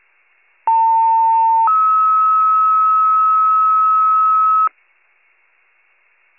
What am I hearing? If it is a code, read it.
Two-tone page, heard as a one-second tone, then a three-second tone, about 900 then 1290 Hz. It is heard from 0.7 to 4.7 s.
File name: pager / 78 RESCUE DUTY / wcalert